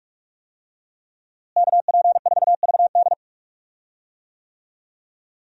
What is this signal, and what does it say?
Morse code, heard as a keyed tone, about 42 wpm, 700 Hz. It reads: KP4VD
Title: DXCC-100 Call Signs - 5 Characters - Repeated 3x - 45wpm